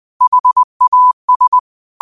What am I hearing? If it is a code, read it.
Morse code, heard as a keyed tone, about 20 wpm, 1000 Hz. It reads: HAS
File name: qotdfiles / 20wpm